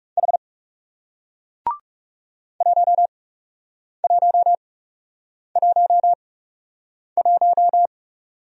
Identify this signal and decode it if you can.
Morse code, heard as a keyed tone, about 40 wpm, spaced as for 8 wpm, 700 Hz. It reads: H 1111